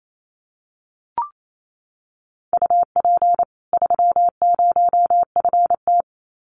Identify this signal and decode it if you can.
Morse code, heard as a keyed tone, about 28 wpm, 700 Hz. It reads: UP 30FT
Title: Common QSO Phrases 28wpm